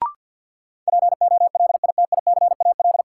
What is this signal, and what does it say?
Morse code, heard as a keyed tone, about 51 wpm, 700 Hz. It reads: POLITICAL
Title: Sentences from Top 800 Words - New Word Drill - 50wpm